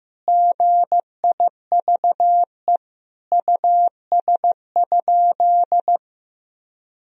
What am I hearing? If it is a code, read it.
Morse code, heard as a keyed tone, about 15 wpm, 700 Hz. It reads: GIVE US?
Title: Sentences from Top 1000 Words - Review Part 4 - 15wpm